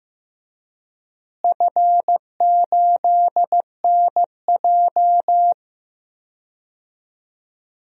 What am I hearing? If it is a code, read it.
Morse code, heard as a keyed tone, about 15 wpm, 700 Hz. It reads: F8NJ
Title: DXCC-100 Call Signs - 4 Characters - Repeated 3x - 15wpm